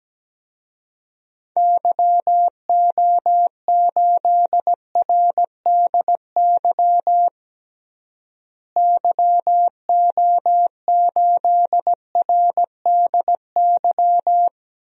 Morse code, heard as a keyed tone, about 17 wpm, 700 Hz. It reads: YO8RDY YO8RDY
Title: DXCC-100 Call Signs - 6 Characters - Repeated 3x - 17wpm